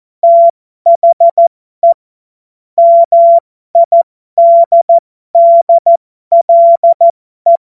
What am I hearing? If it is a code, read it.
Morse code, heard as a keyed tone, about 14 wpm, spaced as for 12 wpm, 680 Hz. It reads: THE MIDDLE
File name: Morsekurs / Witze